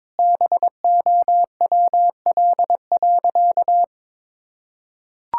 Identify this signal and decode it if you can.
Morse code, heard as a keyed tone, about 22 wpm, 690 Hz. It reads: BOWL.
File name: practice / 22wpm / Easy Sentences with Hints - 22wpm